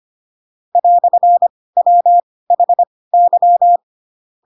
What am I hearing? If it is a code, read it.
Morse code, heard as a keyed tone, about 25 wpm, spaced as for 18 wpm, 700 Hz. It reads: "WHY
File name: Pinocchio / Pinocchio ch7 12 25 ik1hge